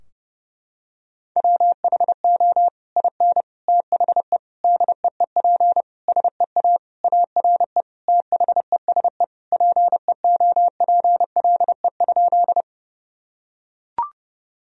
Morse code, heard as a keyed tone, about 30 wpm, 700 Hz. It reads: WHO IN THE DEEP SEA ARE THESE PEOPLE?
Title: Sentences from Top 1000 Words - Review Part 3 - 30wpm